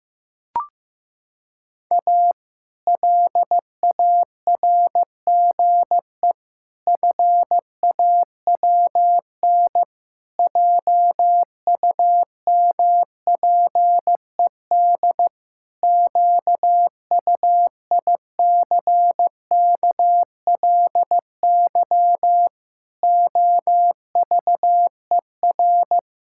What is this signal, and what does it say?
Morse code, heard as a keyed tone, about 15 wpm, 700 Hz. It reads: A LARGE FAWN JUMPED QUICKLY OVER
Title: Pangram Challenge v2 - 15wpm